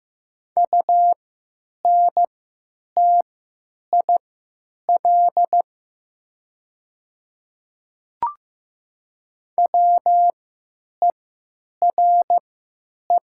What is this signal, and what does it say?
Morse code, heard as a keyed tone, about 15 wpm, spaced as for 9 wpm, 700 Hz. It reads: UNTIL WERE
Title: 3x Character Spacing - Top 300 Words - 15wpm